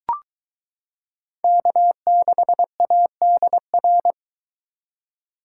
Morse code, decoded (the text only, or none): K6ADR